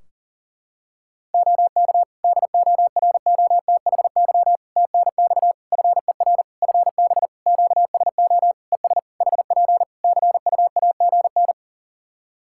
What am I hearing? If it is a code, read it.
Morse code, heard as a keyed tone, about 40 wpm, 700 Hz. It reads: OK DOROTHY TNX FER FB QSO ES HP CUAGN
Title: Practice QSO - Exchange 4 - Ending - 40wpm